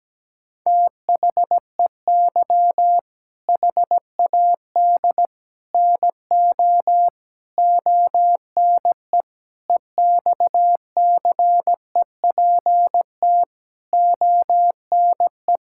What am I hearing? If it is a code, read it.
Morse code, heard as a keyed tone, about 17 wpm, 700 Hz. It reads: THEY HAD NO ONE EXCEPT ONE
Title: Sentences from Top 1600 Words 17wpm